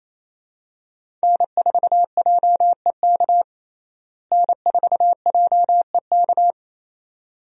Morse code, heard as a keyed tone, about 28 wpm, 700 Hz. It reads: N4JEK N4JEK
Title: USA Call Signs - 5 Characters - Repeated 2x - 28wpm